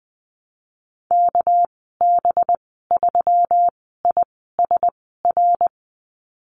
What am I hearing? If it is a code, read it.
Morse code, heard as a keyed tone, about 20 wpm, spaced as for 14 wpm, 700 Hz. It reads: KB3ISR
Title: 207 Lesson 33g - Letter j - Call Signs 20wpm